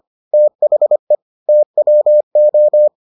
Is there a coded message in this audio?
Yes: Morse code from 0.3 to 2.9 s, heard as a keyed tone, about 25 wpm, 600 Hz. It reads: THE TWO